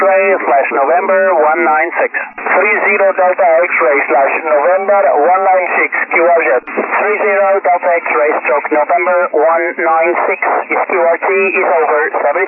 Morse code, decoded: none